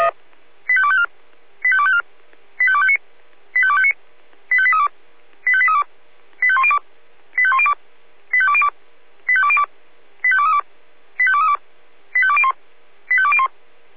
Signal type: five-tone selective call